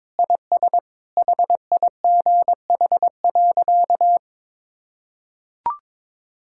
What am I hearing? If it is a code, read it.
Morse code, heard as a keyed tone, about 21 wpm, 700 Hz. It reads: IS HIGH.
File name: practice / 22wpm / Sentences with 3-Word Phrases - 22wpm